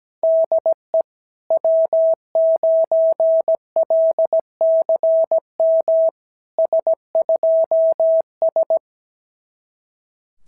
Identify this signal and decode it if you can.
Morse code, heard as a keyed tone, about 17 wpm, 690 Hz. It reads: DE W9LCM S2S
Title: SOTA QSO Practice 17wpm